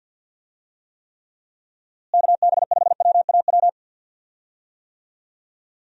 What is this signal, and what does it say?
Morse code, heard as a keyed tone, about 46 wpm, 700 Hz. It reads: KB5WAW
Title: USA Call Signs - 6 Characters - Repeated 3x - 50wpm